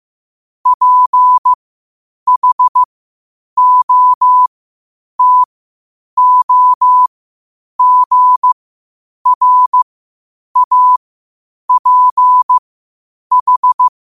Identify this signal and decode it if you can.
Morse code, heard as a keyed tone, about 15 wpm, spaced as for 8 wpm, 1000 Hz. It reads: PHOTOGRAPH
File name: qotdfiles / 8wpm